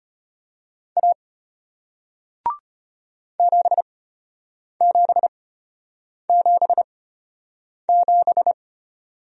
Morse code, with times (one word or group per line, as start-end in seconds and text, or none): 1.0-1.1 A
3.4-8.5 7777